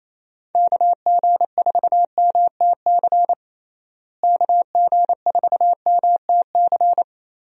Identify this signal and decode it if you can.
Morse code, heard as a keyed tone, about 28 wpm, 700 Hz. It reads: KG4MTC KG4MTC
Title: USA Call Signs - 6 Characters - Repeated 3x - 28wpm